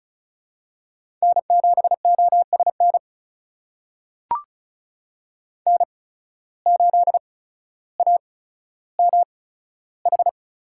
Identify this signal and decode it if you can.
Morse code, heard as a keyed tone, about 36 wpm, spaced as for 15 wpm, 700 Hz. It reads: N7OSN N8AMH
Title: USA Call Signs - 5 Characters - 8x Character Spacing - 35wpm